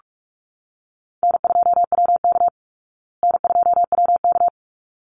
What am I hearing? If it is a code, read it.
Morse code, heard as a keyed tone, about 44 wpm, 700 Hz. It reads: N2WK N2WK
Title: Call Signs Repeated 3x - 45wpm